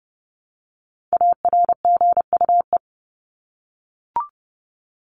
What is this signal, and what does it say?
Morse code, heard as a keyed tone, about 30 wpm, 700 Hz. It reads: ARGUE
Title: Top 1501-1750 Words 30wpm